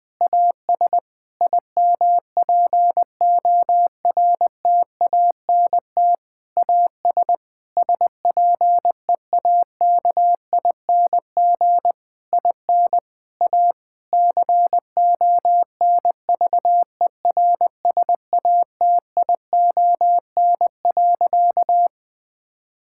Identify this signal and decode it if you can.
Morse code, heard as a keyed tone, about 20 wpm, 700 Hz. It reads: AS IMPORTANT AS SPEAKING IN A CONVERSATION.
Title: Difficult Sentences with Hints - 20wpm